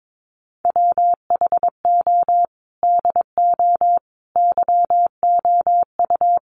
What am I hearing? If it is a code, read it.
Morse code, heard as a keyed tone, about 22 wpm, 700 Hz. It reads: WHO DO YOU
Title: Sentences from Top 1500 Words 22wpm